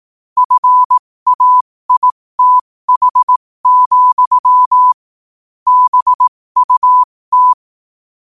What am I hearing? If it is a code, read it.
Morse code, heard as a keyed tone, about 18 wpm, spaced as for 15 wpm, 1000 Hz. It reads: FAITH, BUT
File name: qotdfiles / 15wpm